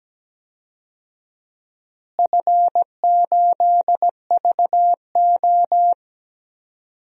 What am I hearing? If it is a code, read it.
Morse code, heard as a keyed tone, about 17 wpm, 700 Hz. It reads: F8VO
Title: DXCC-100 Call Signs - 4 Characters - Repeated 3x - 17wpm